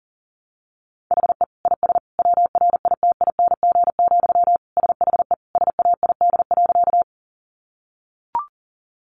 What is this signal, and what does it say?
Morse code, heard as a keyed tone, about 37 wpm, 700 Hz. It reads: HE IS WRITING, SHE SAID.
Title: Sentences from Top 800 Words 40wpm